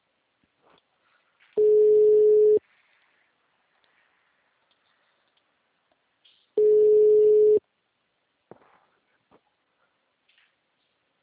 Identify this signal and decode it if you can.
Ringback tone (Europe, 425 Hz).
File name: out- -2013- - - .6681